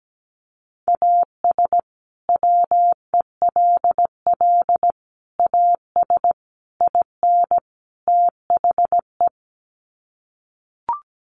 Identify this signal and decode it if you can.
Morse code, heard as a keyed tone, about 17 wpm, 700 Hz. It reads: AS WELL AS IN THE
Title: Sets of 5 Words Encore 17wpm